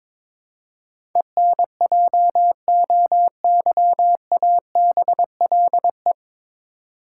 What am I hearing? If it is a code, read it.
Morse code, heard as a keyed tone, about 21 wpm, 700 Hz. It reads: ENJOYABLE
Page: Morse Code Ninja practice 22wpm